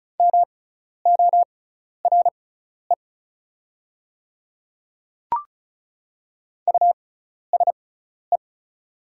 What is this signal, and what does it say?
Morse code, heard as a keyed tone, about 36 wpm, spaced as for 12 wpm, 700 Hz. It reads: MORE USE
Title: 6x Character Spacing - Top 200 Words - 35wpm